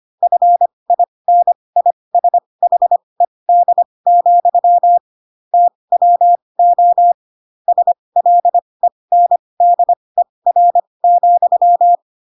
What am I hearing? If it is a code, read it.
Morse code, heard as a keyed tone, about 25 wpm, spaced as for 20 wpm, 700 Hz. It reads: FINISHED, TWO SLENDER,
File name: Pinocchio / Pinocchio ch8 15 25 ik1hge